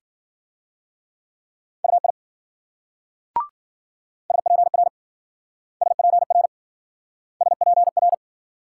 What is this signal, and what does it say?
Morse code, heard as a keyed tone, about 52 wpm, 700 Hz. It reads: UI SPR SPR SPR